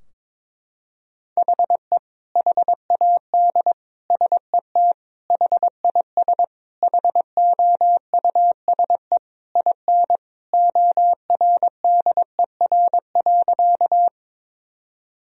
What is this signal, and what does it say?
Morse code, heard as a keyed tone, about 22 wpm, 700 Hz. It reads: HE HAD SET HIS HOUSE IN ORDER.
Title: Sentences from Top 1000 Words - Review Part 2 - 22wpm